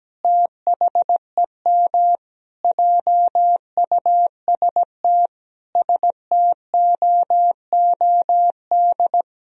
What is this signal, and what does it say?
Morse code, heard as a keyed tone, about 17 wpm, 700 Hz. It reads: THEM JUST STOOD
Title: Sentences from Top 1000 Words - Review Part 2 - 17wpm